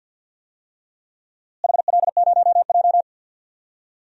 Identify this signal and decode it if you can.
Morse code, heard as a keyed tone, about 48 wpm, 700 Hz. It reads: HL0J
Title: DXCC-100 Call Signs - 4 Characters - Repeated 3x - 50wpm